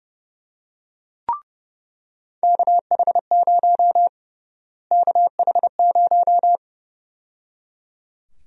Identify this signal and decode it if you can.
Morse code, heard as a keyed tone, about 30 wpm, 700 Hz. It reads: KH0 KH0
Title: USA Call Sign Prefixes - Repeated 2x - 30wpm